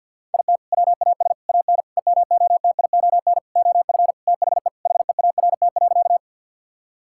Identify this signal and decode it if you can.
Morse code, heard as a keyed tone, about 50 wpm, 700 Hz. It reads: IT WAS AN EMOTION OF THE HEART.
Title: Sentences from Top 1600 Words 50wpm